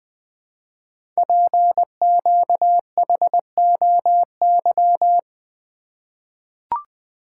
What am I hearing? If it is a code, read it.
Morse code, heard as a keyed tone, about 20 wpm, 700 Hz. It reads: PQHOY
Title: Code Groups - 5 Letters - 20wpm